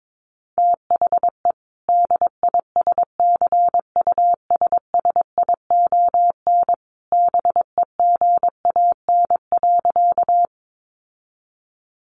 Morse code, heard as a keyed tone, about 22 wpm, 700 Hz. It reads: THE DISCUSSION BEGAN.
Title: Sentences from Top 1100 Words 22wpm